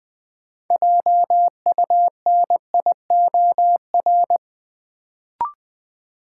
Morse code, heard as a keyed tone, about 20 wpm, 700 Hz. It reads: JUNIOR
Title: Sentences from Top 1500 Words - New Word Drill Alt - 20wpm